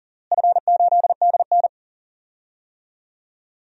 Morse code, heard as a keyed tone, about 40 wpm, 700 Hz. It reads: F8DN
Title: DXCC-100 Call Signs - 4 Characters - Repeated 2x - 40wpm